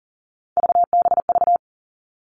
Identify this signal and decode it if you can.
Morse code, heard as a keyed tone, about 40 wpm, 700 Hz. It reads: VBV
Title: All Cut Numbers - 3-Digit 40wpm